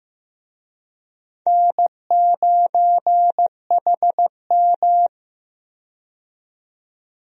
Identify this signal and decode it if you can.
Morse code, heard as a keyed tone, about 15 wpm, 700 Hz. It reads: N9HM
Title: USA Call Signs - 4 Characters - 15wpm